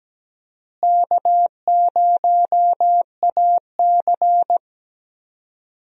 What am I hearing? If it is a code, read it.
Morse code, heard as a keyed tone, about 17 wpm, 700 Hz. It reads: K0AC
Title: USA Call Signs - 4 Characters - Repeated 2x - 17wpm